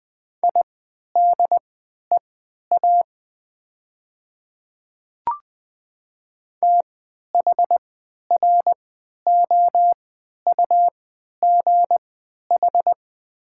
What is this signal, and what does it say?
Morse code, heard as a keyed tone, about 20 wpm, spaced as for 11 wpm, 700 Hz. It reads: IDEA THROUGH
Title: 3x Character Spacing - Top 300 Words - 20wpm